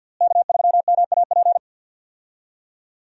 Morse code, heard as a keyed tone, about 46 wpm, 700 Hz. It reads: K3MAP
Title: USA Call Signs - 5 Characters - Repeated 2x - 50wpm